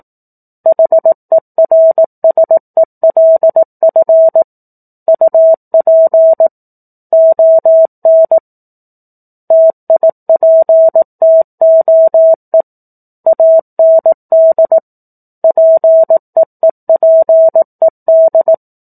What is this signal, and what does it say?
Morse code, heard as a keyed tone, about 18 wpm, 650 Hz. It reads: HERSELF UP ON TIPTOE AND PEEPED
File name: 04-Chapter-04 18 WPM